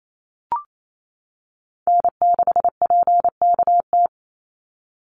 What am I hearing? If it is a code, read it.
Morse code, heard as a keyed tone, about 28 wpm, 700 Hz. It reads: N6PKT
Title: NA SOTA Chaser Callsigns 28wpm